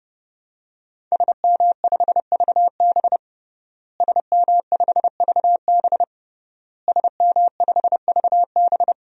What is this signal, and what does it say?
Morse code, heard as a keyed tone, about 30 wpm, 700 Hz. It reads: SM5VB SM5VB SM5VB
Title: DXCC-100 Call Signs - 5 Characters - Repeated 3x - 30wpm